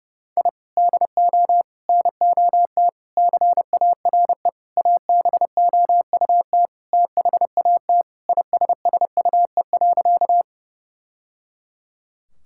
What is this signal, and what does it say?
Morse code, heard as a keyed tone, about 30 wpm, 700 Hz. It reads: I DO NOT CARE ABOUT THAT ISSUE.
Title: Sentences from Top 1000 Words - Review Part 6 - 30wpm